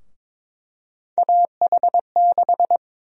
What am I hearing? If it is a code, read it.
Morse code, heard as a keyed tone, about 22 wpm, 700 Hz. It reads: AH6
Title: USA Call Sign Prefixes - Repeated 2x - 22wpm